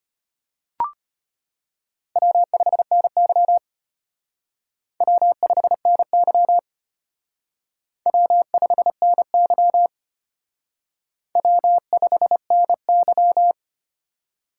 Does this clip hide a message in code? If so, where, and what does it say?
Yes: Morse code from 2.2 to 13.5 s, heard as a keyed tone, about 32 wpm, 700 Hz. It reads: W5NY W5NY W5NY W5NY